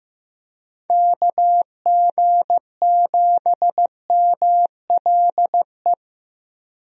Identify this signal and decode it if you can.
Morse code, heard as a keyed tone, about 15 wpm, 700 Hz. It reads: KG7MLE
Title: USA Call Signs - 6 Characters - Repeated 2x - 15wpm